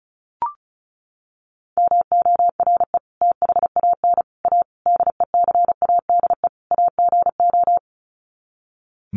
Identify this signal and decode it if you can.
Morse code, heard as a keyed tone, about 34 wpm, 700 Hz. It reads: MORE THAN A DECADE AGO
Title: Sets of 5 Words Encore2 35wpm